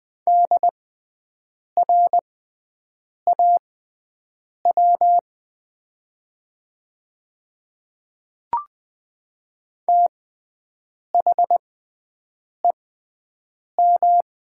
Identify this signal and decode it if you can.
Morse code, heard as a keyed tone, about 20 wpm, spaced as for 7 wpm, 700 Hz. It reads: DRAW THEM
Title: 6x Character Spacing - Top 300 Words - 20wpm